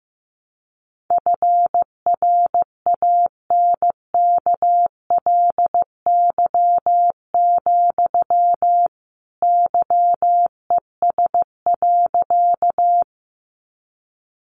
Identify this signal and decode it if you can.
Morse code, heard as a keyed tone, about 15 wpm, 700 Hz. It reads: FRANKLY, YES.
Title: A Sound Of Thunder 15wpm